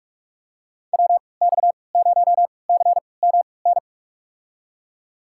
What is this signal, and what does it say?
Morse code, heard as a keyed tone, about 46 wpm, spaced as for 26 wpm, 700 Hz. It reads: WX0CMN
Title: USA Call Signs - 6 Characters - 3x Character Spacing - 45wpm